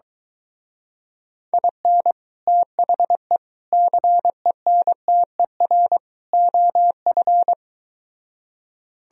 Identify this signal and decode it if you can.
Morse code, heard as a keyed tone, about 23 wpm, 700 Hz. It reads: IN THE CENTER OF